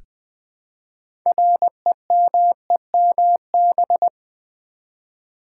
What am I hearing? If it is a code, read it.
Morse code, heard as a keyed tone, about 20 wpm, 700 Hz. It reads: REMEMB
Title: Build A Word - 3 Letters At A Time - 8-Letter Words - 20wpm